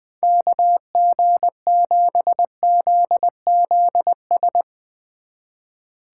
Morse code, heard as a keyed tone, about 20 wpm, 700 Hz. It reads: KG7ZZS
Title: Speed Racing - USA 2x3 Call Signs 20wpm